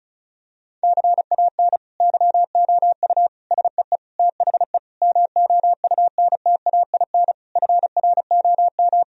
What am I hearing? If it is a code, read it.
Morse code, heard as a keyed tone, about 34 wpm, 700 Hz. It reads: CAN YOU SEE THE MOUNTAIN FROM
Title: Sentences from Top 1000 Words - Review Part 1 - 35wpm